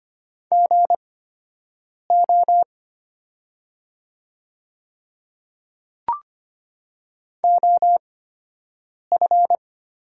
Morse code, heard as a keyed tone, about 25 wpm, spaced as for 7 wpm, 700 Hz. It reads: GO OF
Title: 8x Character Spacing - Top 100 Words - 25wpm